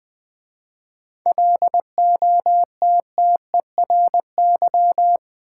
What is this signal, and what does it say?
Morse code, heard as a keyed tone, about 20 wpm, 700 Hz. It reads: LOTTERY
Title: 7 Letter Words - First 3 Letters Given - 20wpm